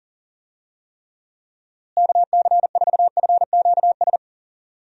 Morse code, heard as a keyed tone, about 40 wpm, 700 Hz. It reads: KC4FQS
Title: USA Call Signs - 6 Characters - Repeated 3x - 40wpm